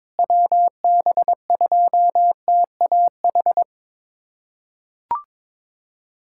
Morse code, heard as a keyed tone, about 22 wpm, 700 Hz. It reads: WB2TAH
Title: USA Callsign DE Callsign - 6 Characters - 22wpm